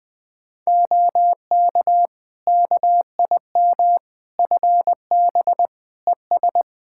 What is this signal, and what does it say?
Morse code, heard as a keyed tone, about 20 wpm, 700 Hz. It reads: OK KIM FB ES
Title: Practice QSO - Exchange 2 - Info - 20wpm